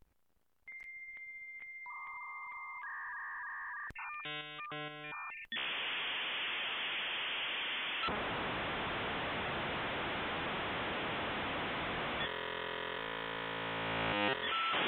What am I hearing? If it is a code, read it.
Dial-up modem handshake.